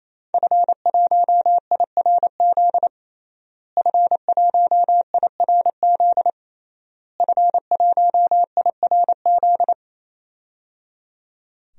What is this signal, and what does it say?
Morse code, heard as a keyed tone, about 28 wpm, 700 Hz. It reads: F1IRZ F1IRZ F1IRZ
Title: DXCC-100 Call Signs - 5 Characters - Repeated 3x - 28wpm